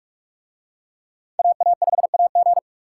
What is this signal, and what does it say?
Morse code, heard as a keyed tone, about 46 wpm, 700 Hz. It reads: AA5AG